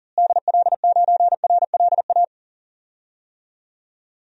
Morse code, heard as a keyed tone, about 40 wpm, 700 Hz. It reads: DL9RLA
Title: DXCC-100 Call Signs - 6 Characters - Repeated 3x - 40wpm